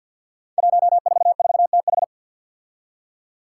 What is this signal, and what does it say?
Morse code, heard as a keyed tone, about 51 wpm, 700 Hz. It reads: 144TH
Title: Five Challenging Letter-Number Combinations 50wpm